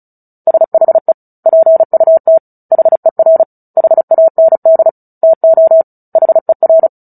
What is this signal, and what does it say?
Morse code, heard as a keyed tone, about 34 wpm, 650 Hz. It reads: SHE PUT HER HAND TO HER
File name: Book Two / 09-Chapter 9 - 35 WPM